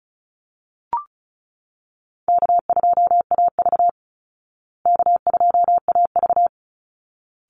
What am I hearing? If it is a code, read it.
Morse code, heard as a keyed tone, about 34 wpm, 700 Hz. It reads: K2AV K2AV
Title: Easy Call Signs Repeated 2x - 35wpm